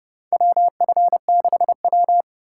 Morse code, heard as a keyed tone, about 31 wpm, 700 Hz. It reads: WF6W
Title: USA Call Signs - 4 Characters - 2x Character Spacing - 30wpm